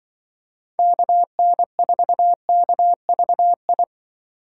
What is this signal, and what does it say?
Morse code, heard as a keyed tone, about 26 wpm, 700 Hz. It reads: KN4KVI